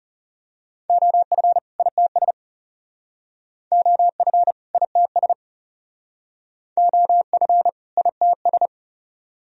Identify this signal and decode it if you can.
Morse code, heard as a keyed tone, about 37 wpm, 700 Hz. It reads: OF ITS OF ITS OF ITS
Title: Speed-Racing - Sets of 2 Words 30wpm